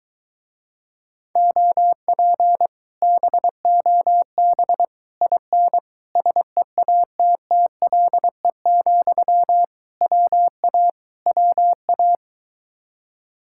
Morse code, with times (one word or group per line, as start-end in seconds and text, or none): 1.4-2.7 OP
3.0-4.8 BOB
5.2-5.8 IN
6.2-9.6 SEATTLE,
10.0-10.9 WA
11.3-12.2 WA